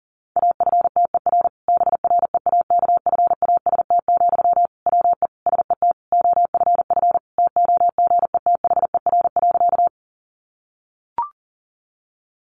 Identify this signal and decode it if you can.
Morse code, heard as a keyed tone, about 40 wpm, 700 Hz. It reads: AFTER BREAKFAST, WE SET OFF TOGETHER.
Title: Sentences from Top 1500 Words 40wpm